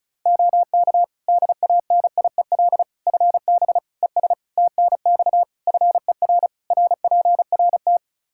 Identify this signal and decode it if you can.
Morse code, heard as a keyed tone, about 36 wpm, 700 Hz. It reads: OK DANIEL FB ES TNX FER RPRT
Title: K1USN SST QSO Practice 35wpm